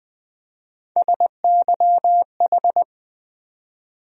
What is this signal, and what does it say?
Morse code, heard as a keyed tone, about 20 wpm, 700 Hz. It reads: SYH
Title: USA Call Sign Postfixes - 3 Letters 20wpm